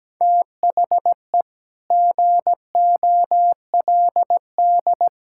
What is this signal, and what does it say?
Morse code, heard as a keyed tone, about 17 wpm, 700 Hz. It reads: THE GOLD
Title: Sentences from Top 1000 Words - Review Part 5 - 17wpm